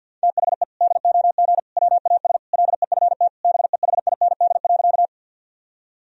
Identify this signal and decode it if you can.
Morse code, heard as a keyed tone, about 52 wpm, 700 Hz. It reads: THE DOG WAS LEFT BEHIND.
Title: Sentences from Top 1000 Words - Review Part 2 - 50wpm